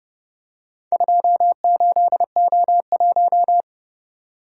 Morse code, heard as a keyed tone, about 29 wpm, 700 Hz. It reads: 28O1